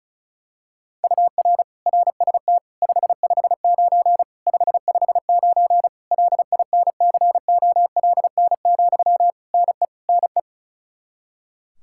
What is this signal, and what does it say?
Morse code, heard as a keyed tone, about 34 wpm, 700 Hz. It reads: UR RST 559 559 LINCOLN, NE NE
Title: Practice QSO - Exchange 1 - Beginning - 35wpm